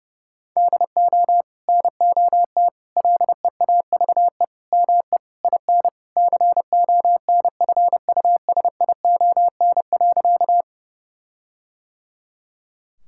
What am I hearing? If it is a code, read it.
Morse code, heard as a keyed tone, about 30 wpm, 700 Hz. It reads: DO NOT LEAVE ME IN CONFUSION.
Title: Sentences from Top 1600 Words 30wpm